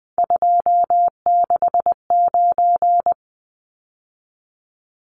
Morse code, heard as a keyed tone, about 20 wpm, 700 Hz. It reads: 269
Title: Numbers - 3-Digit 20wpm